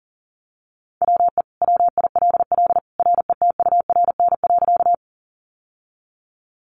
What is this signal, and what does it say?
Morse code, heard as a keyed tone, about 40 wpm, 690 Hz. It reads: WE WILL RETURN.
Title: Sentences from Top 800 Words 40wpm